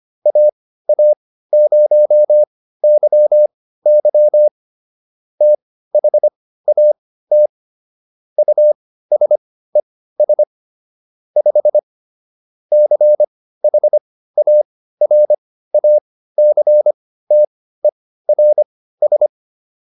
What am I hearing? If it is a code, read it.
Morse code, heard as a keyed tone, about 25 wpm, spaced as for 15 wpm, 600 Hz. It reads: AA0YY THAT USES 5 CHARACTERS